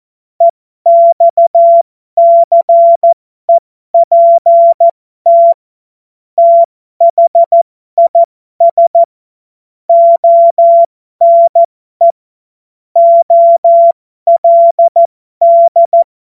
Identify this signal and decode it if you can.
Morse code, heard as a keyed tone, about 14 wpm, spaced as for 12 wpm, 680 Hz. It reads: EXCEPT THIS ONE OLD
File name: Morsekurs / Witze